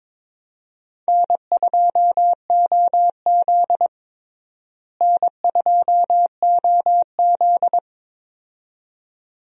Morse code, heard as a keyed tone, about 22 wpm, 700 Hz. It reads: N2OZ N2OZ
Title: USA Call Signs - 4 Characters - Repeated 2x - 22wpm